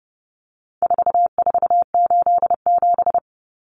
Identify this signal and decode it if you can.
Morse code, heard as a keyed tone, about 30 wpm, 700 Hz. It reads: 4487
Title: Numbers - 4-Digit 30wpm